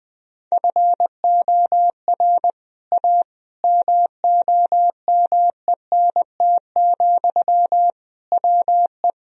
Morse code, heard as a keyed tone, about 21 wpm, 700 Hz. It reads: FOR A MOMENT, WE
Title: Sentences from Top 1000 Words - Review Part 2 - 20wpm